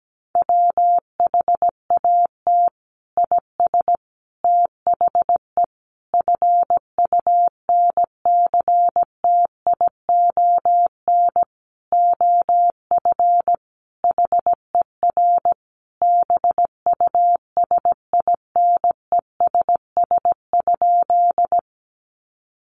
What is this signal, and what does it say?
Morse code, heard as a keyed tone, about 17 wpm, 700 Hz. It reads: WHAT IS THE FUNCTION OF HER BUSINESS?
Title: Sentences from Top 1100 Words 17wpm